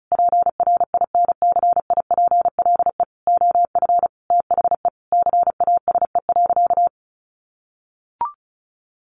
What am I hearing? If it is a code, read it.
Morse code, heard as a keyed tone, about 34 wpm, 700 Hz. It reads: PRINCIPLE OF THE CASE.
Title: Sentences from Top 1400 Words 35wpm